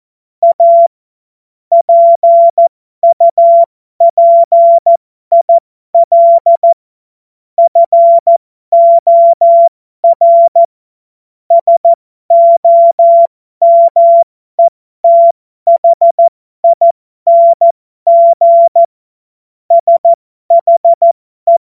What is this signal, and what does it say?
Morse code, heard as a keyed tone, about 14 wpm, spaced as for 12 wpm, 680 Hz. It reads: A PUPIL FOR SOMETHING SHE
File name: Morsekurs / Witze